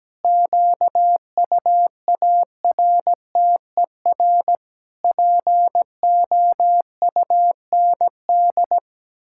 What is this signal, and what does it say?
Morse code, heard as a keyed tone, about 17 wpm, 700 Hz. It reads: QUARTER POUND
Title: Pangram Challenge v2 - 17wpm